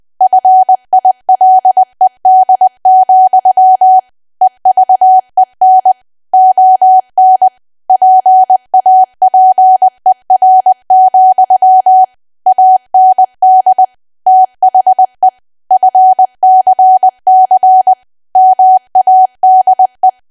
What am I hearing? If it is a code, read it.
Morse code, heard as a keyed tone, about 20 wpm, 750 Hz. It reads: FILED, EVEN ON PAPER, AND THE FCC MADE
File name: cwa / W1AW-Level3 / 20WPM